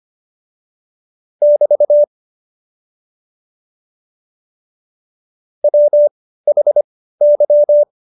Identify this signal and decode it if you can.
Morse code, heard as a keyed tone, about 25 wpm, spaced as for 15 wpm, 590 Hz. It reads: <BT> WHY
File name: Jokes-25-char-15-eff